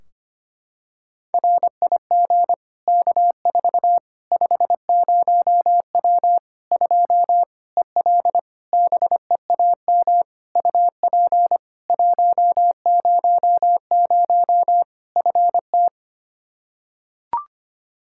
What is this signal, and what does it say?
Morse code, heard as a keyed tone, about 25 wpm, 700 Hz. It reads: RIG K4 50W 2 EL BEAM UP 100 FT
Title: Practice QSO - Exchange 2 - Info - 25wpm